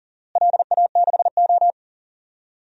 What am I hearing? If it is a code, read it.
Morse code, heard as a keyed tone, about 40 wpm, 700 Hz. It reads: LA6O
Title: DXCC-100 Call Signs - 4 Characters - Repeated 3x - 40wpm